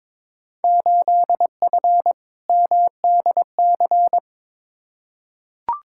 Morse code, heard as a keyed tone, about 22 wpm, 700 Hz. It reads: 8F MDC
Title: Field Day Exchanges Encore - 22wpm